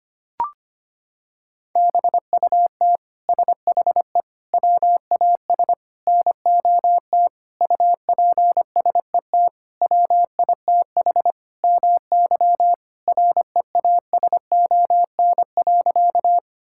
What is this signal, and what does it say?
Morse code, heard as a keyed tone, about 25 wpm, 700 Hz. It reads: BUT SHE WAS NOT UPSET WITH MY REASON.
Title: Sentences from Top 1000 Words - Review Part 5 - 25wpm